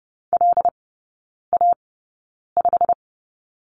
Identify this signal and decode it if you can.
Morse code, heard as a keyed tone, about 30 wpm, spaced as for 9 wpm, 700 Hz. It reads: LA5
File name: course / 30:10wpm / 087 Lesson 15e - Number 5 - Three Characters 30wpm